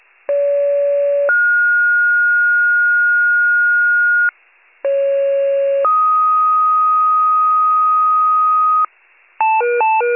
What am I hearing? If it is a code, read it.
Two-tone page, heard as a one-second tone, then a three-second tone, about 570 then 1430 Hz, about 550 then 1250 Hz, with noise. It is heard from 0.3 to 8.9 s.